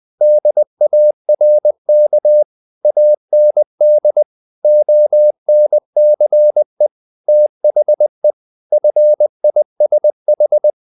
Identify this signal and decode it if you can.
Morse code, heard as a keyed tone, about 20 wpm, 600 Hz. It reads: DARK AND ONCE THE FISH
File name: the old man and the sea / the old man and the sea part 1 20wpm